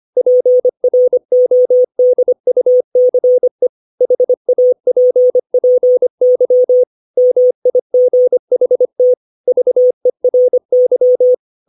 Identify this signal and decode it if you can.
Morse code, heard as a keyed tone, about 25 wpm, 500 Hz. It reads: PRODUCE HAPPY MIGHT VERY
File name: 632 words / 632 Words 25WPM